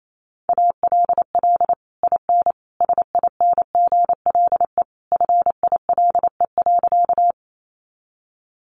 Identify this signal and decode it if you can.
Morse code, heard as a keyed tone, about 28 wpm, 700 Hz. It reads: ALL IN SINGLE FILE.
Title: Sentences from Top 1200 Words 28wpm